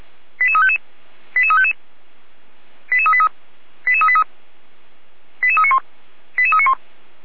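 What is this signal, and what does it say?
Five-tone selective call (ZVEI), heard as quick runs of five tones, tones of about 70 ms. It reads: 88355 88355 88383 88383 88391 88391
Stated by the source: ZVEI five-tone selective call received on an SDR.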